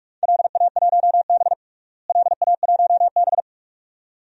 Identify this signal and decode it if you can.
Morse code, heard as a keyed tone, about 42 wpm, 700 Hz. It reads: LA1B LA1B
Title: DXCC-100 Call Signs - 4 Characters - Repeated 2x - 45wpm